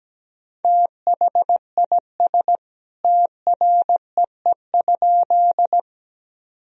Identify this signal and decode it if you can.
Morse code, heard as a keyed tone, about 17 wpm, 700 Hz. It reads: THIS TREE?
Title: Sentences from Top 1000 Words - Review Part 1 - 17wpm